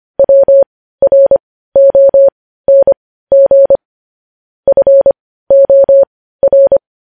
Morse code, heard as a keyed tone, about 25 wpm, spaced as for 15 wpm, 550 Hz. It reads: WRONG FOR
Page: news headlines in morse code at 15 wpm